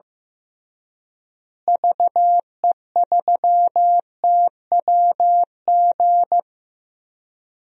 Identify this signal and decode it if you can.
Morse code, heard as a keyed tone, about 15 wpm, 700 Hz. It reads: VE3TWG
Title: DXCC-100 Call Signs - 6 Characters - Repeated 3x - 15wpm